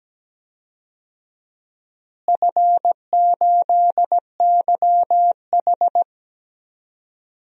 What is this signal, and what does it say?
Morse code, heard as a keyed tone, about 17 wpm, 700 Hz. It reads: F8YH
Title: DXCC-100 Call Signs - 4 Characters - Repeated 3x - 17wpm